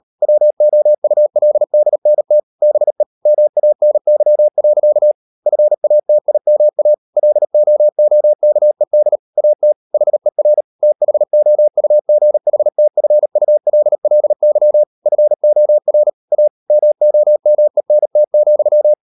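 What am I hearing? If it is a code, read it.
Morse code, heard as a keyed tone, about 38 wpm, 600 Hz. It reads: WOULDNT BE MANY. FATIMA LOOKED AT HER THOUGHTFULLY FOR A MOMENT,
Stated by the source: Morse code practice at 38 WPM, sending Ken Follett's novel Never in English, from CW med Gnister.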